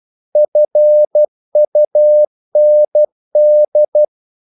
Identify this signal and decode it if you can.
Morse code, heard as a keyed tone, about 12 wpm, 600 Hz. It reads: FUND
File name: Korte ord DK 12wpm